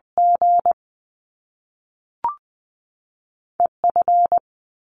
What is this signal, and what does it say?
Morse code, heard as a keyed tone, about 20 wpm, spaced as for 3 wpm, 700 Hz. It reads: GE F
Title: Top Two Letter Combinations in Words 20wpm